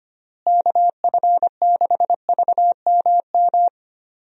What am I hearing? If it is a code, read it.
Morse code, heard as a keyed tone, about 25 wpm, 700 Hz. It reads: KF6VMM
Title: USA Call Signs - 6 Characters - Repeated 3x - 25wpm